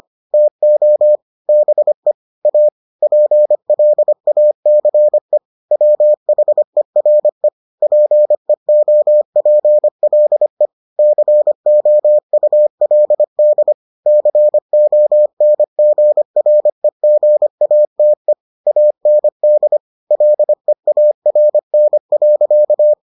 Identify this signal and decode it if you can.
Morse code, heard as a keyed tone, about 25 wpm, 600 Hz. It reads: TO BE A PLACE WHERE PEOPLE COULD CONGREGATE AND LEARN.